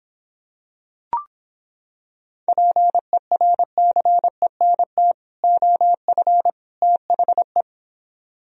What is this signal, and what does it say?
Morse code, heard as a keyed tone, about 26 wpm, 700 Hz. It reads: PERCENT OF THE